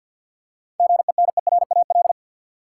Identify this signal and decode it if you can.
Morse code, heard as a keyed tone, about 47 wpm, 700 Hz. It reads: GENERAL
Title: Seven Letter Words - 50wpm